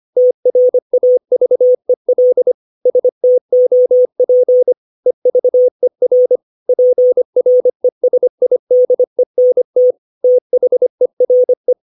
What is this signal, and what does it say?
Morse code, heard as a keyed tone, about 25 wpm, 500 Hz. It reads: TRAVEL STOP EVER PRESIDENT THERE
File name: 632 words / 632 Words 25WPM